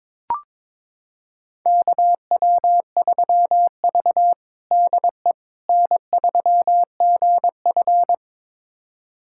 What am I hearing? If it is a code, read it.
Morse code, heard as a keyed tone, about 22 wpm, 700 Hz. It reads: KW3V DE N3GF
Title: USA Callsign DE Callsign - 4 Characters - 22wpm